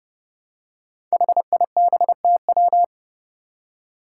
Morse code, heard as a keyed tone, about 31 wpm, 700 Hz. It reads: HIBTW
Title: Five Challenging Letter-Number Combinations 30wpm